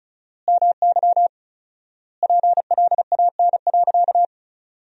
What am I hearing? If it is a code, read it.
Morse code, heard as a keyed tone, about 36 wpm, 700 Hz. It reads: MY PLAN.
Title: 4x Word-Spacing - Sentences from Top 500 Words - 35wpm